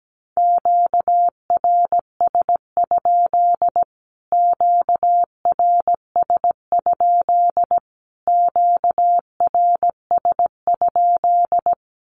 Morse code, heard as a keyed tone, about 17 wpm, 700 Hz. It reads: QRS? QRS? QRS?
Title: Q-Codes Repeated 3x 17wpm